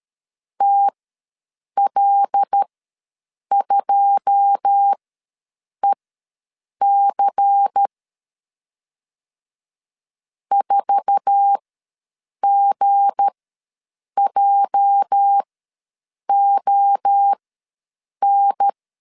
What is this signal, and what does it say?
Morse code, heard as a keyed tone, about 13 wpm, spaced as for 7 wpm, 800 Hz. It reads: TL2EC 4GJON
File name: LEZIONI / 9 2 ws 12wpm 3406